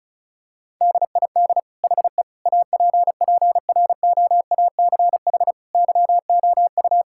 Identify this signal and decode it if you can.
Morse code, heard as a keyed tone, about 36 wpm, 700 Hz. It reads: DID HE APPROACH YOU
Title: Sentences from Top 1000 Words - Review Part 6 - 35wpm